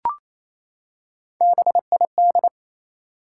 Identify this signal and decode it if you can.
Morse code, heard as a keyed tone, about 27 wpm, 700 Hz. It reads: BID
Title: Three Letter Words - 28wpm